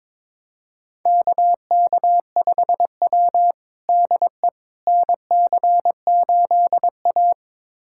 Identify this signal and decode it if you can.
Morse code, heard as a keyed tone, about 21 wpm, 700 Hz. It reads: KK5W DE NC8A
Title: USA Callsign DE Callsign - 4 Characters - 22wpm